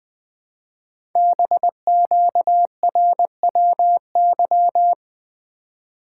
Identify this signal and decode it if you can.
Morse code, heard as a keyed tone, about 20 wpm, 700 Hz. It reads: BQRWY
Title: Code Groups - 5 Letters - 20wpm